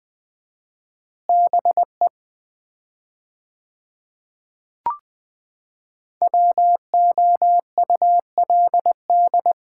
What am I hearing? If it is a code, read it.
Morse code, heard as a keyed tone, about 20 wpm, 700 Hz. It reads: BE WOULD
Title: 4x Word-Spacing - Sets of 3 Words - 20wpm